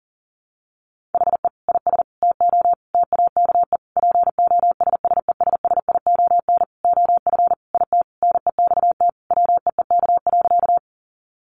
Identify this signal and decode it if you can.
Morse code, heard as a keyed tone, about 40 wpm, 700 Hz. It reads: HE IS TO TAKE POSSESSION OF IT NEXT WEEK.
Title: Sentences from Top 1400 Words 40wpm